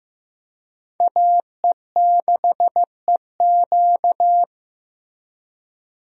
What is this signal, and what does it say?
Morse code, heard as a keyed tone, about 15 wpm, 700 Hz. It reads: AE6EQ
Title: USA Call Signs - 5 Characters - Repeated 3x - 15wpm